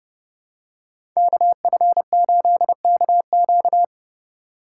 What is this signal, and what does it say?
Morse code, heard as a keyed tone, about 30 wpm, 700 Hz. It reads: KF8KQ
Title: USA Call Signs - 2x Character Spacing - 30wpm